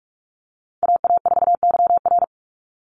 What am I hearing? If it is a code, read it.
Morse code, heard as a keyed tone, about 44 wpm, 700 Hz. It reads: AA4YR
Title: USA Call Signs 45wpm